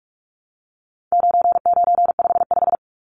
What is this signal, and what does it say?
Morse code, heard as a keyed tone, about 44 wpm, 700 Hz. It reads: 9955
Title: Numbers - 4-Digit 45wpm